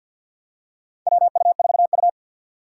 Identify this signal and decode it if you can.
Morse code, heard as a keyed tone, about 50 wpm, 700 Hz. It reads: WU4U
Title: USA Call Signs - 4 Characters - 2x Character Spacing - 50wpm